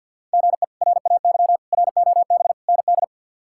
Morse code, heard as a keyed tone, about 52 wpm, 690 Hz. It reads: GE RAY ROB ND
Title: K1USN SST QSO Practice - 50wpm